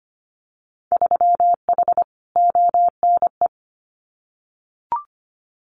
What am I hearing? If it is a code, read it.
Morse code, heard as a keyed tone, about 25 wpm, 700 Hz. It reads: 3H ONE
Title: Winter Field Day Exchanges Encore 25wpm